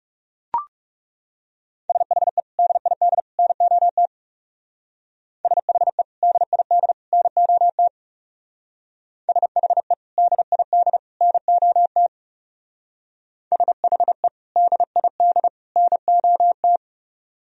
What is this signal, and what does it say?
Morse code, heard as a keyed tone, about 38 wpm, 700 Hz. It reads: SHE DID NOT SHE DID NOT SHE DID NOT SHE DID NOT